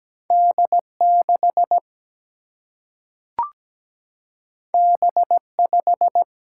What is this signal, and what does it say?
Morse code, heard as a keyed tone, about 17 wpm, 700 Hz. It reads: D6 B5
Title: Single-Letter Single-Number 17wpm